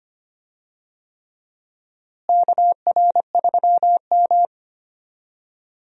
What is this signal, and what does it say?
Morse code, heard as a keyed tone, about 25 wpm, 700 Hz. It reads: KR3M
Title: USA Call Signs - 4 Characters - Repeated 3x - 25wpm